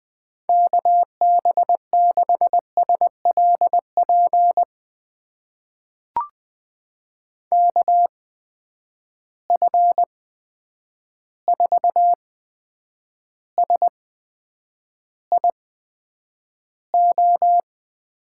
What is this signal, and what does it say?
Morse code, heard as a keyed tone, about 20 wpm, spaced as for 9 wpm, 700 Hz. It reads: KB6SLP KF4SIO
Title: USA Call Signs - 6 Characters - 8x Character Spacing - 20wpm